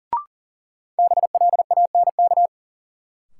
Morse code, heard as a keyed tone, about 40 wpm, 700 Hz. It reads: BLANK
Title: Sentences from Top 1500 Words - New Word Drill - 40wpm